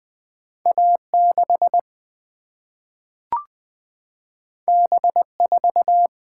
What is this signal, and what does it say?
Morse code, heard as a keyed tone, about 20 wpm, 700 Hz. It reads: A 6 B 4
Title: Single-Letter Single-Number 20wpm